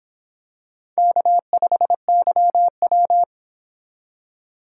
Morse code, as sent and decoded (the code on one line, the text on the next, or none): -.- ..... -.-- .--
K5YW